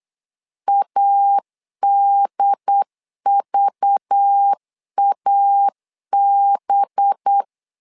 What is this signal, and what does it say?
Morse code, heard as a keyed tone, about 8 wpm, 800 Hz. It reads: ADVAB
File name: LEZIONI / 5 LBVC / 5 LBV 8wpm 2837